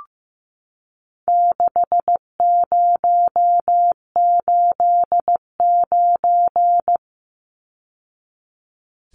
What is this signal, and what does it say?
Morse code, heard as a keyed tone, about 15 wpm, 700 Hz. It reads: 6089
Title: Numbers - 4-Digit 15wpm